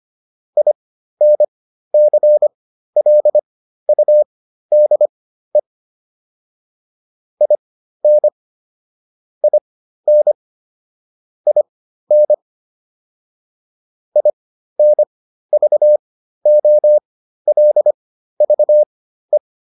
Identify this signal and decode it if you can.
Morse code, heard as a keyed tone, about 25 wpm, spaced as for 13 wpm, 600 Hz. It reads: INCLUDE IN IN IN INVOLVE
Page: intermediate-practice-files